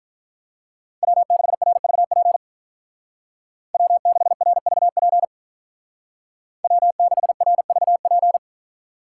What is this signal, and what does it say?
Morse code, heard as a keyed tone, about 43 wpm, 700 Hz. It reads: W6RVP W6RVP W6RVP
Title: Speed Racing - USA Call Signs 35wpm